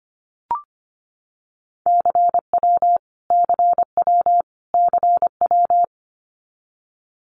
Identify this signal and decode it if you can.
Morse code, heard as a keyed tone, about 25 wpm, 700 Hz. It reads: CW CW CW
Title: Abbreviations Repeated 3x - 25wpm